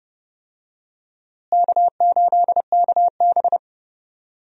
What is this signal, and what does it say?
Morse code, heard as a keyed tone, about 30 wpm, 700 Hz. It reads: K8KB
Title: USA Call Signs - 4 Characters - Repeated 3x - 30wpm